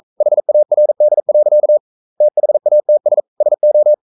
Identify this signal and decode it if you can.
Morse code, heard as a keyed tone, about 42 wpm, 600 Hz. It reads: HARD. THATS SO
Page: Romanen Never af Ken Follet på engelsk.